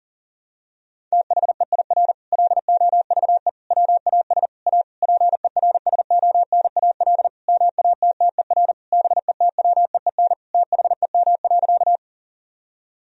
Morse code, heard as a keyed tone, about 40 wpm, 700 Hz. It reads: THEIR LOVE WAS A PERSONAL MATTER BETWEEN THEM.
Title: Sentences from Top 1000 Words - Review Part 4 - 40wpm